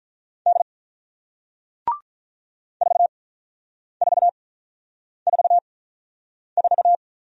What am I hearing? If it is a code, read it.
Morse code, heard as a keyed tone, about 48 wpm, spaced as for 9 wpm, 700 Hz. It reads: D 4444